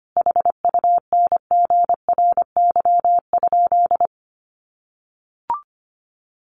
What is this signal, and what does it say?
Morse code, heard as a keyed tone, about 25 wpm, 700 Hz. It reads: HUNGRY?
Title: Sentences from Top 1400 Words 25wpm